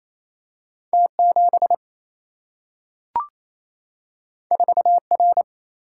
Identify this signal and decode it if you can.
Morse code, heard as a keyed tone, about 28 wpm, 700 Hz. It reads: T 7 4 R
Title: Single-Letter Single-Number 28wpm